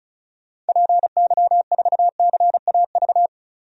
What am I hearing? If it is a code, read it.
Morse code, heard as a keyed tone, about 36 wpm, 700 Hz. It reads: PY4CAV
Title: DXCC-100 Call Signs - 6 Characters - 35wpm